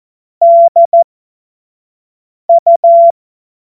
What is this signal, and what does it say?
Morse code, heard as a keyed tone, about 14 wpm, spaced as for 5 wpm, 680 Hz. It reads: DU
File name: Morsekurs / Witze / Witz 08. Februar 2022 25 F70